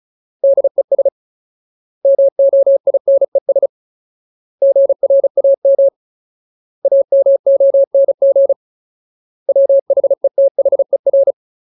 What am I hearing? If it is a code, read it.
Morse code, heard as a keyed tone, about 34 wpm, 550 Hz. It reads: DES MOINES GRAM AMONG WHETHER
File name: Ultimate Word List / 35WPM Ultimate